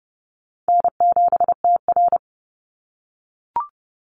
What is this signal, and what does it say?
Morse code, heard as a keyed tone, about 30 wpm, 700 Hz. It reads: N7TR
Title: Easy Call Signs - 30wpm